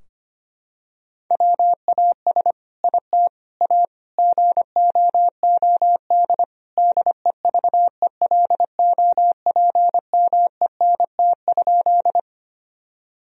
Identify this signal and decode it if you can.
Morse code, heard as a keyed tone, about 25 wpm, 700 Hz. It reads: WAS IT A GOOD DEVELOPMENT?
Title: Sentences from Top 1000 Words - Review Part 4 - 25wpm